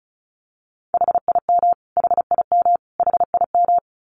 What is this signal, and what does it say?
Morse code, heard as a keyed tone, about 34 wpm, 700 Hz. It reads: HIM HIM HIM
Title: Top 200 Words Repeated 35wpm